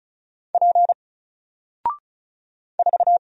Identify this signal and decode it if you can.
Morse code, heard as a keyed tone, about 36 wpm, 700 Hz. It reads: P 4
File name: Single-Letter-Number-35wpm